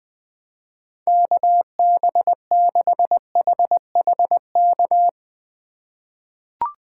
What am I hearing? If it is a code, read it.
Morse code, heard as a keyed tone, about 20 wpm, 700 Hz. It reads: KB6HHK
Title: USA Call Signs - 6 Characters - 4x Character Spacing - 20wpm